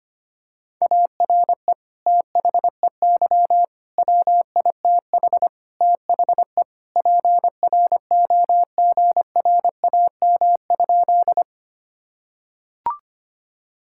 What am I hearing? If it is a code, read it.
Morse code, heard as a keyed tone, about 25 wpm, 700 Hz. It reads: ARE THEY WITH THE PROGRAM?
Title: Sentences from Top 1000 Words - Review Part 5 - 25wpm